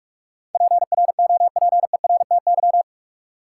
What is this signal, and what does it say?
Morse code, heard as a keyed tone, about 46 wpm, 700 Hz. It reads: PROPERTY
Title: Sentences from Top 1000 Words - New Word Drill - 45wpm